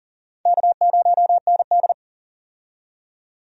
Morse code, heard as a keyed tone, about 40 wpm, 690 Hz. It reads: K0ND
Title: USA Call Signs - 4 Characters - 40wpm